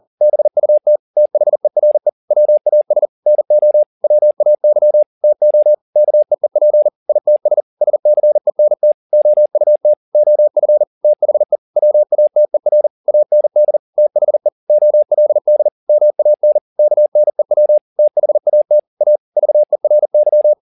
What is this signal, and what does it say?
Morse code, heard as a keyed tone, about 40 wpm, 600 Hz. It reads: BUT THERE WAS NO WAY TO KEEP ITS SCENT OUT OF THE WATER AND THE OLD MAN KNEW THAT A VERY
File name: the old man and the sea 2 40wpm